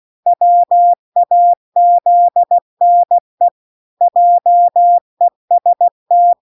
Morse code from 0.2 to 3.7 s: WAZNE